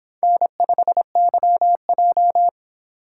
Morse code, heard as a keyed tone, about 26 wpm, 700 Hz. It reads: N5YJ